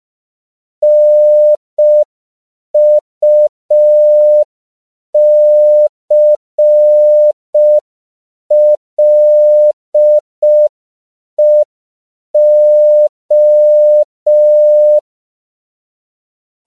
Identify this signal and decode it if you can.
Morse code, heard as a keyed tone, about 5 wpm, 600 Hz. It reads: NUCLEO
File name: CW / 5PPM